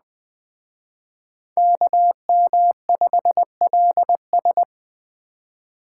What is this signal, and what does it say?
Morse code, heard as a keyed tone, about 20 wpm, 700 Hz. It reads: KM5LS
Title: Speed Racing - USA 2x2 Call Signs 17wpm